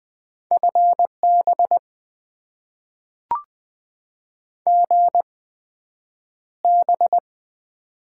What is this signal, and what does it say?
Morse code, heard as a keyed tone, about 20 wpm, spaced as for 9 wpm, 700 Hz. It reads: FB GB
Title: Top 90 QSO Elements - 8x Character Spacing - 20wpm